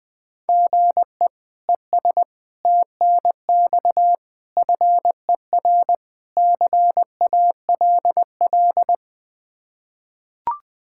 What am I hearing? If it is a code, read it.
Morse code, heard as a keyed tone, about 20 wpm, 700 Hz. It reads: GE ES TNX FER CALL
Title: K1USN SST QSO Practice 20wpm